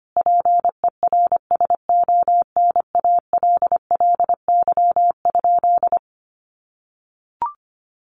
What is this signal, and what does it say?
Morse code, heard as a keyed tone, about 25 wpm, 700 Hz. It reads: PERSONALLY?
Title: Sentences from Top 1400 Words 25wpm